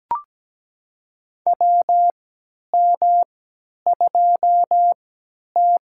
Morse code, heard as a keyed tone, about 17 wpm, spaced as for 10 wpm, 700 Hz. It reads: WM2T
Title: USA Call Signs - 4 Characters - 3x Character Spacing - 17wpm